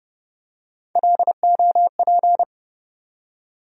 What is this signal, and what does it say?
Morse code, heard as a keyed tone, about 32 wpm, 700 Hz. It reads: LOP
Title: USA Call Sign Postfixes - 3 Letters 30wpm